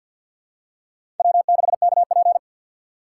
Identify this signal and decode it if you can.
Morse code, heard as a keyed tone, about 50 wpm, 700 Hz. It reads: W6KP
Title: USA Call Signs - 4 Characters - 3x Character Spacing - 50wpm